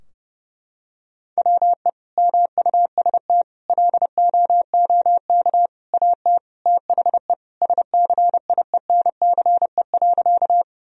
Morse code, heard as a keyed tone, about 30 wpm, 700 Hz. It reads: WE MUST LOOK AT THE SCIENCE.
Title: Sentences from Top 1000 Words - Review Part 2 - 30wpm